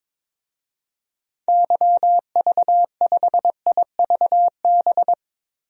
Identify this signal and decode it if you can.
Morse code, heard as a keyed tone, about 22 wpm, 700 Hz. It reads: YV5IVB
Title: DXCC-100 Call Signs - 6 Characters - Repeated 3x - 22wpm